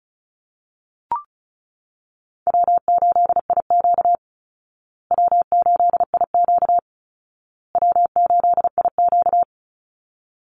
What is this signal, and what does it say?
Morse code, heard as a keyed tone, about 34 wpm, 700 Hz. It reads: W8IQ W8IQ W8IQ
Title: Easy Call Signs Repeated 3x - 35wpm